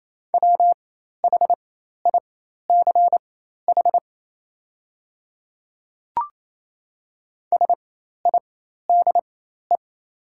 Morse code, heard as a keyed tone, about 28 wpm, spaced as for 13 wpm, 700 Hz. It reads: WHICH SIDE
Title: 4x Character Spacing - Top 100 Words - 28wpm